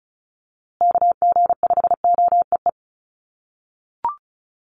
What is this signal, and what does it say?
Morse code, heard as a keyed tone, about 34 wpm, 700 Hz. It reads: KG5OEE
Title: USA Call Signs Repeated 2x 35wpm